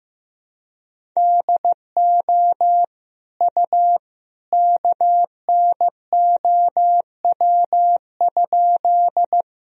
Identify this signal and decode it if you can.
Morse code, heard as a keyed tone, about 15 wpm, 700 Hz. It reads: DO U KNOW?
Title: Sentences from Top 100 Words and Abbreviations 15wpm